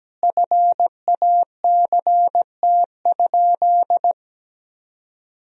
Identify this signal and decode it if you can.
Morse code, heard as a keyed tone, about 17 wpm, 700 Hz. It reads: FACT?
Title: Sentences from Top 1000 Words - Review Part 3 - 17wpm